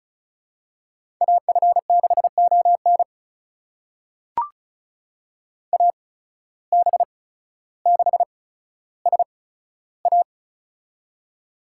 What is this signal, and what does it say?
Morse code, heard as a keyed tone, about 34 wpm, spaced as for 15 wpm, 700 Hz. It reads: AF6ON AB6SA
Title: USA Call Signs - 5 Characters - 8x Character Spacing - 35wpm